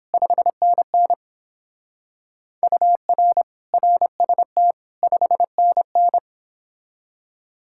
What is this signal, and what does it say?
Morse code, heard as a keyed tone, about 31 wpm, 700 Hz. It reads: 5NN UR RST 5NN